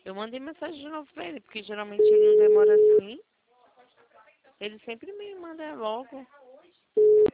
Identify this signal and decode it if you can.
Ringback tone (Europe, 425 Hz), with speech.